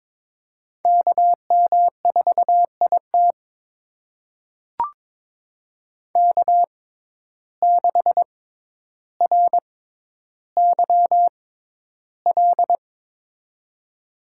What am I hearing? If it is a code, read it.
Morse code, heard as a keyed tone, about 22 wpm, spaced as for 11 wpm, 700 Hz. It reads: KM4IT K6RYL
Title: USA Call Signs - 5 Characters - 6x Character Spacing - 22wpm